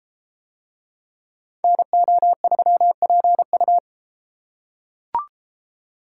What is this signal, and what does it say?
Morse code, heard as a keyed tone, about 31 wpm, 700 Hz. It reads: NO3PU